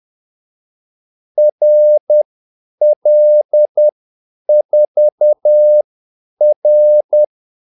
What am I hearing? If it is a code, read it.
Morse code, heard as a keyed tone, about 10 wpm, spaced as for 8 wpm, 600 Hz. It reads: RL4R
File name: Callsigns-10-8wpm